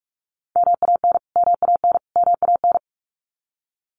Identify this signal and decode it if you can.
Morse code, heard as a keyed tone, about 44 wpm, 700 Hz. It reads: MAN MAN MAN
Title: Top 200 Words Repeated 45wpm